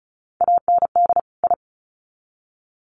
Morse code, heard as a keyed tone, about 34 wpm, 700 Hz. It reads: AND I
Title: Sets of 2 Words 35wpm